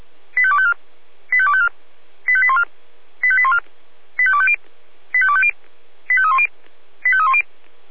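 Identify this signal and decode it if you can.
Five-tone selective call (ZVEI), heard as quick runs of five tones, tones of about 70 ms. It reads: 85254 85254 85814 85814 85250 85250 85310 85310
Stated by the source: SDR-received ZVEI five-tone call 85304.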